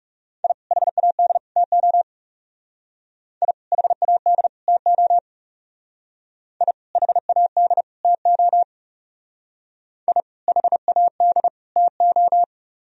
Morse code, heard as a keyed tone, about 34 wpm, 700 Hz. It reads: I HAD TO I HAD TO I HAD TO I HAD TO
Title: Speed-Racing - Sets of 3 Words 30wpm